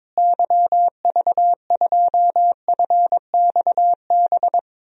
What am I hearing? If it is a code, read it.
Morse code, heard as a keyed tone, about 21 wpm, 700 Hz. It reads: YV2FXB
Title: DXCC-100 Call Signs - 6 Characters - Repeated 3x - 22wpm